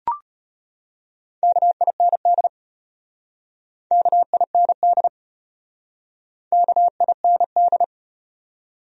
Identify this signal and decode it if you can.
Morse code, heard as a keyed tone, about 35 wpm, 700 Hz. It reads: KIND KIND KIND